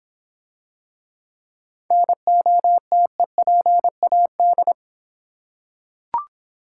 Morse code, heard as a keyed tone, about 26 wpm, 700 Hz. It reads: NOTEPAD